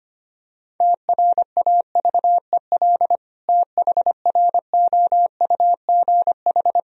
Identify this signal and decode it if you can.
Morse code, heard as a keyed tone, about 25 wpm, 700 Hz. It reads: TRAVEL THROUGH
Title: Sentences from Top 1000 Words - Review Part 2 - 25wpm